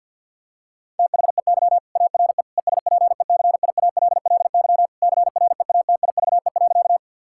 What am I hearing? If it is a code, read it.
Morse code, heard as a keyed tone, about 50 wpm, 700 Hz. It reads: THEY ARE ESPECIALLY CREATIVE.
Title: Sentences from Top 1000 Words - Review Part 5 - 50wpm